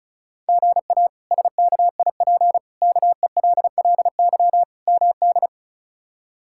Morse code, heard as a keyed tone, about 34 wpm, 700 Hz. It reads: GA SKIP KELLY MD
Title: K1USN SST QSO Practice - 35wpm